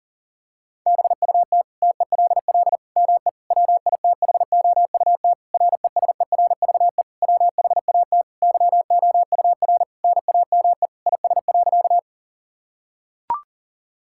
Morse code, heard as a keyed tone, about 37 wpm, 700 Hz. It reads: BUT TELL ME WITHOUT RESERVE WHAT YOUR NAME IS.
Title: Sentences from Top 1000 Words - Review Part 6 - 40wpm